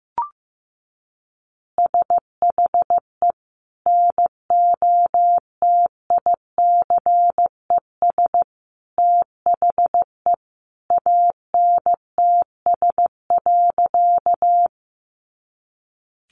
Morse code, heard as a keyed tone, about 15 wpm, 700 Hz. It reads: SHE NOTICES THE ANTS.
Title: Additional Easy Phrases 15wpm